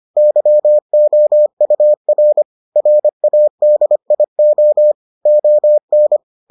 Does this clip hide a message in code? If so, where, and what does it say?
Yes: Morse code from 0.2 to 6.2 s, heard as a keyed tone, about 25 wpm, 600 Hz. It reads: YOUR RADIO ON